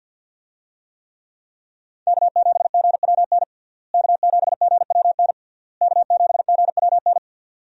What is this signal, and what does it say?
Morse code, heard as a keyed tone, about 50 wpm, 700 Hz. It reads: K7GWN K7GWN K7GWN
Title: USA Call Signs - 5 Characters - Repeated 3x - 50wpm